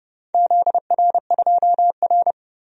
Morse code, heard as a keyed tone, about 30 wpm, 700 Hz. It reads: ZR2R
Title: DXCC-100 Call Signs - 4 Characters - 30wpm